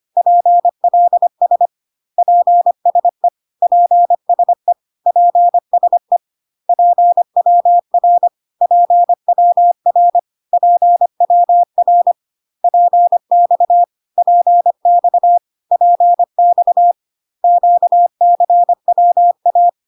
Morse code, heard as a keyed tone, about 25 wpm, 700 Hz. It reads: PLS PSE PSE PSE PWR PWR PWR PX PX PX QCWA
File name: CW abbreviations alphabetical repeated 3 times 25 25wpm